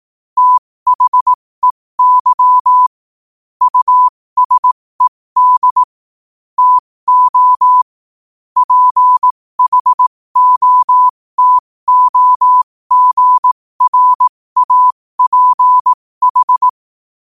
Morse code, heard as a keyed tone, about 18 wpm, spaced as for 15 wpm, 1000 Hz. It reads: THEY USED TO PHOTOGRAPH
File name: qotdfiles / 15wpm